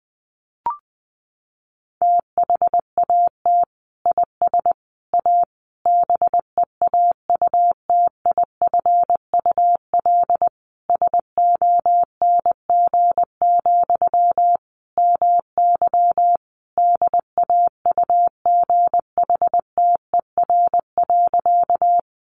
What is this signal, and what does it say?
Morse code, heard as a keyed tone, about 20 wpm, 700 Hz. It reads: THAT IS A BEAUTIFUL SONG, MY DAUGHTER.
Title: Sentences from Top 1500 Words 20wpm